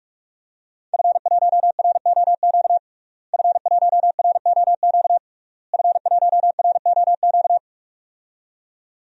Morse code, heard as a keyed tone, about 46 wpm, 700 Hz. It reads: F1ROQ F1ROQ F1ROQ
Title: DXCC-100 Call Signs - 5 Characters - Repeated 3x - 45wpm